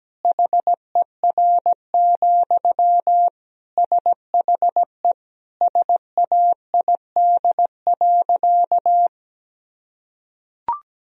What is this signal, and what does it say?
Morse code, heard as a keyed tone, about 17 wpm, 700 Hz. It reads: HER, SHE SAID.
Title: Sentences from Top 1000 Words - Review Part 4 - 17wpm